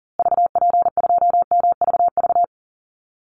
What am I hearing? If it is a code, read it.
Morse code, heard as a keyed tone, about 40 wpm, 700 Hz. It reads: VP2MVV
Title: Call Signs Repeated 2x - 40wpm